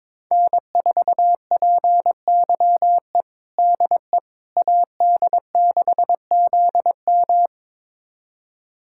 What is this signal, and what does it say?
Morse code, heard as a keyed tone, about 22 wpm, 700 Hz. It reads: N4PYE DE AD6ZM
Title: USA Callsign DE Callsign - 5 Characters - 22wpm